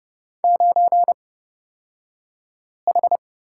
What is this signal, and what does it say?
Morse code, heard as a keyed tone, about 29 wpm, spaced as for 5 wpm, 700 Hz. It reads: 9H
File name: Lesson 37d - Letter x - Two Characters 30wpm